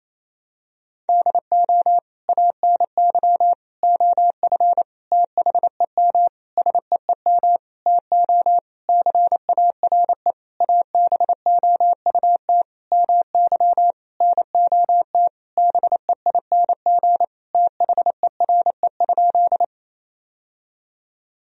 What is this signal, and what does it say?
Morse code, heard as a keyed tone, about 28 wpm, 700 Hz. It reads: DO ANY OF THEM SEEM TO CARE ABOUT MY NOT BEING THERE?
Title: Sentences from Top 1000 Words - Review Part 3 - 28wpm